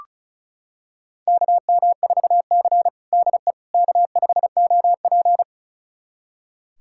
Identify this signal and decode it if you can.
Morse code, heard as a keyed tone, about 36 wpm, 700 Hz. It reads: KM4C DE K5OP
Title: USA Callsign DE Callsign - 4 Characters - 35wpm